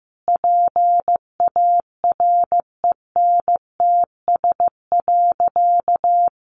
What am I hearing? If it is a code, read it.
Morse code, heard as a keyed tone, about 15 wpm, 700 Hz. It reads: PARENTS.
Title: Easy Phrases 15wpm